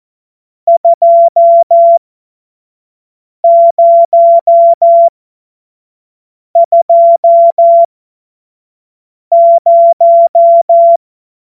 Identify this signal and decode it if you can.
Morse code, heard as a keyed tone, about 14 wpm, spaced as for 5 wpm, 680 Hz. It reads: 2020